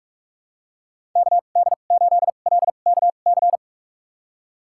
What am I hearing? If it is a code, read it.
Morse code, heard as a keyed tone, about 46 wpm, spaced as for 32 wpm, 700 Hz. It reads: KD8LKC
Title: USA Call Signs - 2x Character Spacing - 45wpm